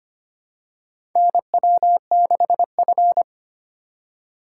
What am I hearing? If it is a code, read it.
Morse code, heard as a keyed tone, about 25 wpm, 700 Hz. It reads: NW6F
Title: USA Call Signs - 4x Character Spacing - 25wpm